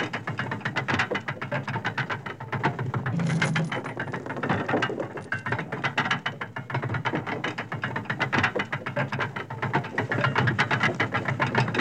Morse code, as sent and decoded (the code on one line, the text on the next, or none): none